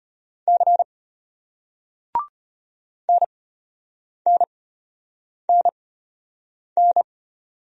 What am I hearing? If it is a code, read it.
Morse code, heard as a keyed tone, about 34 wpm, spaced as for 8 wpm, 700 Hz. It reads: C NNNN